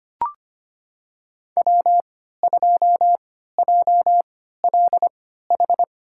Morse code, heard as a keyed tone, about 25 wpm, spaced as for 14 wpm, 700 Hz. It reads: W2JLH
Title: USA Call Signs - 5 Characters - 3x Character Spacing - 25wpm